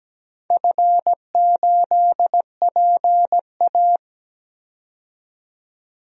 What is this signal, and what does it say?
Morse code, heard as a keyed tone, about 17 wpm, 700 Hz. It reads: F8PA
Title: DXCC-100 Call Signs - 4 Characters - 17wpm